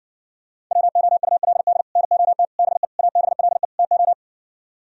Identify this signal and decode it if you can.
Morse code, heard as a keyed tone, about 49 wpm, 700 Hz. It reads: WOULD NOT BE ABLE TO
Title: Speed-Racing - Sets of 5 Words 40wpm